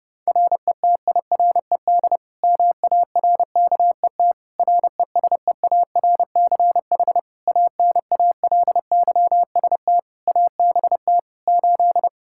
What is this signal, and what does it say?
Morse code, heard as a keyed tone, about 32 wpm, 700 Hz. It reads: RETIRED MARKET RESEARCH ANALYST ABT 8
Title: Practice QSO - Exchange 3 - More Info - 30wpm